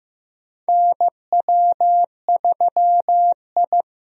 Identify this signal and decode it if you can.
Morse code, heard as a keyed tone, about 15 wpm, 700 Hz. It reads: NW3I
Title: USA Call Signs - 4 Characters - 15wpm